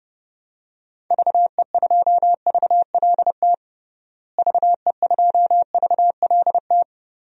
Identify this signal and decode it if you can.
Morse code, heard as a keyed tone, about 29 wpm, 700 Hz. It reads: VE2VLT VE2VLT
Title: DXCC-100 Call Signs - 6 Characters - Repeated 2x - 30wpm